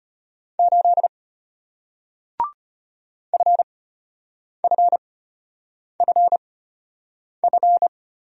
Morse code, heard as a keyed tone, about 33 wpm, spaced as for 8 wpm, 690 Hz. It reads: 8 FFFF